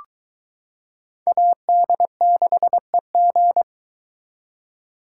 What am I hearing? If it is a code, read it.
Morse code, heard as a keyed tone, about 22 wpm, 700 Hz. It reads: AD6EG